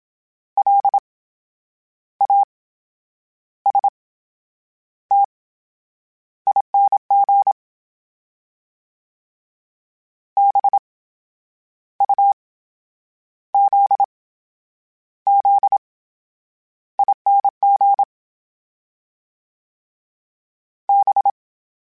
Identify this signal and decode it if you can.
Morse code, heard as a keyed tone, about 27 wpm, spaced as for 9 wpm, 800 Hz. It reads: LASTING BUZZING B
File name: Special INGs / Special INGs-mp3s / VerySlowToFaster Ing @25